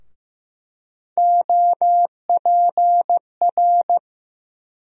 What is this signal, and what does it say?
Morse code, heard as a keyed tone, about 15 wpm, 700 Hz. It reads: OPR
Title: USA Call Sign Postfixes - 3 Letters 15wpm